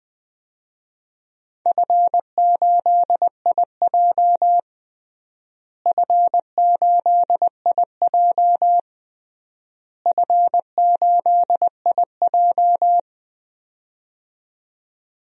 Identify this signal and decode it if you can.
Morse code, heard as a keyed tone, about 21 wpm, 700 Hz. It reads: F8IJ F8IJ F8IJ
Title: DXCC-100 Call Signs - 4 Characters - Repeated 3x - 20wpm